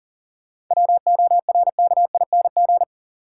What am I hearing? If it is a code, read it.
Morse code, heard as a keyed tone, about 40 wpm, 700 Hz. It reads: WORKING
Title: Sentences from Top 700 Words - New Word Drill - 40wpm